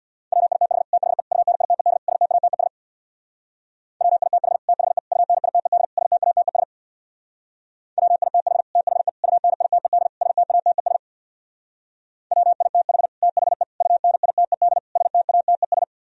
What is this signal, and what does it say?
Morse code, heard as a keyed tone, about 65 wpm, 700 Hz. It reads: WITH THE UNITED STATES WITH THE UNITED STATES WITH THE UNITED STATES WITH THE UNITED STATES